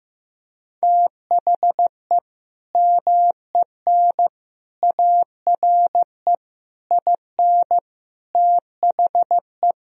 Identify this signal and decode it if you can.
Morse code, heard as a keyed tone, about 15 wpm, 700 Hz. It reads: THE MEN ARE IN THE
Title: Sentences from Top 1000 Words - Review Part 1 - 15wpm